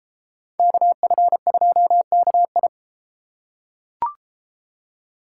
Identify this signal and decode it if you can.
Morse code, heard as a keyed tone, about 33 wpm, 700 Hz. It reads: KF2KI